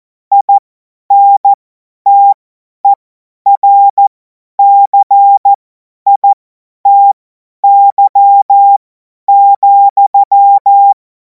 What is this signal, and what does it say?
Morse code, heard as a keyed tone, about 14 wpm, spaced as for 10 wpm, 800 Hz. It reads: INTERCITY,
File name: Morsekurs / Witze / Witz 10. Februar 2026 50 F70